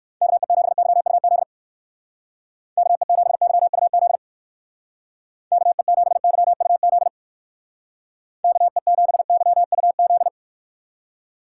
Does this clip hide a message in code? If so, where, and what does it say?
Yes: Morse code from 0.2 to 10.3 s, heard as a keyed tone, about 59 wpm, 700 Hz. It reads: KE7YUZ KE7YUZ KE7YUZ KE7YUZ